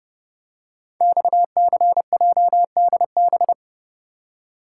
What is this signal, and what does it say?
Morse code, heard as a keyed tone, about 30 wpm, 700 Hz. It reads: XCJDB
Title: Code Groups - 5 Letters - 30wpm